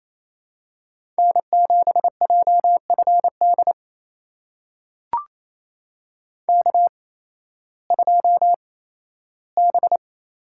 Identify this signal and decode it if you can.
Morse code, heard as a keyed tone, about 28 wpm, spaced as for 15 wpm, 700 Hz. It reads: N7JFD K2B
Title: USA Call Signs - 5 Characters - 8x Character Spacing - 28wpm